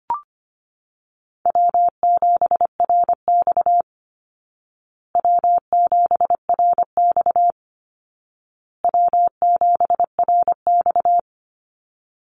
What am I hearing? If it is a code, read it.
Morse code, heard as a keyed tone, about 25 wpm, 700 Hz. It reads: W7RX W7RX W7RX
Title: Call Signs Repeated 2x - 25wpm